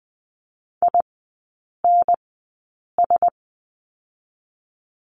Morse code, heard as a keyed tone, about 20 wpm, spaced as for 8 wpm, 700 Hz. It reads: INS
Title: 041 Lesson 8e - Number 1 - Three Characters 20wpm